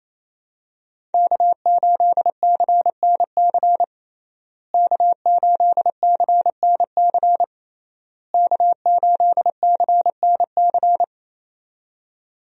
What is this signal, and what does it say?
Morse code, heard as a keyed tone, about 27 wpm, 700 Hz. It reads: K8CNC K8CNC K8CNC
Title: USA Call Signs - 5 Characters - Repeated 3x - 28wpm